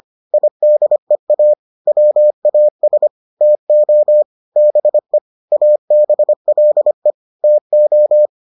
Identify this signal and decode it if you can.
Morse code, heard as a keyed tone, about 25 wpm, 600 Hz. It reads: IDEA WAS TO BE ABLE TO